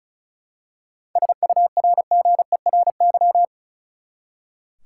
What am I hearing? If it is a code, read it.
Morse code, heard as a keyed tone, about 34 wpm, 700 Hz. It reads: SURGERY
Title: Seven Letter Words - 35wpm